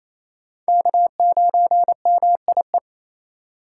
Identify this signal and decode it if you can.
Morse code, heard as a keyed tone, about 28 wpm, 700 Hz. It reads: K9MIE
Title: USA Call Signs - 5 Characters - 4x Character Spacing - 28wpm